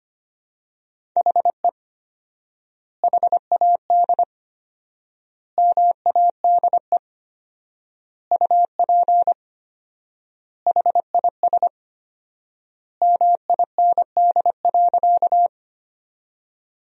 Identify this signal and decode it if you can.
Morse code, heard as a keyed tone, about 25 wpm, 700 Hz. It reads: HE HAD MADE UP HIS MIND.
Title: 4x Word-Spacing - Sentences from Top 500 Words - 25wpm